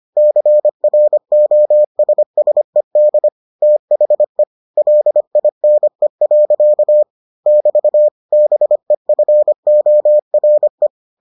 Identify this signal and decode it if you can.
Morse code, heard as a keyed tone, about 25 wpm, 600 Hz. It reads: CROSSED THE LINE. <BT> BEFORE